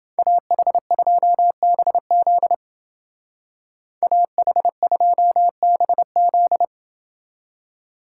Morse code, heard as a keyed tone, about 28 wpm, 700 Hz. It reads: AH2BZ AH2BZ